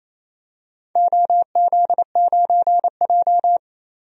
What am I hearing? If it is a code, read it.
Morse code, heard as a keyed tone, about 28 wpm, 700 Hz. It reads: OZ9J
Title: DXCC-100 Call Signs - 4 Characters - 28wpm